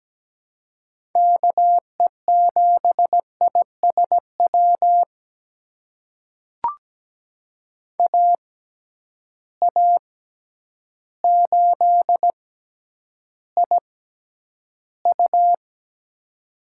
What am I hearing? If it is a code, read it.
Morse code, heard as a keyed tone, about 17 wpm, spaced as for 9 wpm, 700 Hz. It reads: KE7ISW AA8IU
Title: USA Call Signs - 6x Character Spacing - 17wpm